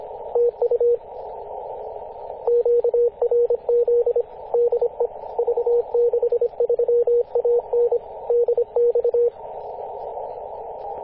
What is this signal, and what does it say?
Morse code, heard as a keyed tone, about 26 wpm, 470 Hz, with radio noise. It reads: TU QRZ DE V63AN DX
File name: posts / 2016-june-08-v63an-14-015mhz-cw